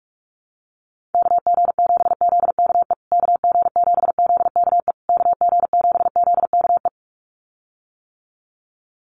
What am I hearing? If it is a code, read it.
Morse code, heard as a keyed tone, about 44 wpm, 700 Hz. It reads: KG7ZKE KG7ZKE KG7ZKE
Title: USA Call Signs Repeated 3x 45wpm